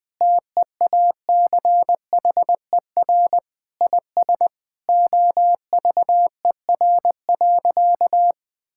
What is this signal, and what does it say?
Morse code, heard as a keyed tone, about 20 wpm, 700 Hz. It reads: TEACHER IS OVER.
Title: Sentences from Top 1600 Words 20wpm